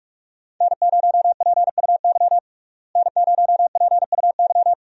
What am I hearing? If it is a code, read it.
Morse code, heard as a keyed tone, about 46 wpm, 700 Hz. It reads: N0PUY N0PUY
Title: USA Call Signs - 5 Characters - Repeated 2x - 45wpm